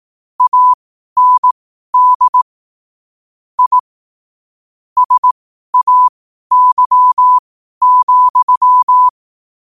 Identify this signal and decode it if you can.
Morse code, heard as a keyed tone, about 18 wpm, spaced as for 12 wpm, 1000 Hz. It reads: AND I SAY,
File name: qotdfiles / 12wpm